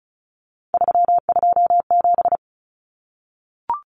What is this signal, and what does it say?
Morse code, heard as a keyed tone, about 34 wpm, 700 Hz. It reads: 327
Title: Common Cut Numbers - 3-Digit 35wpm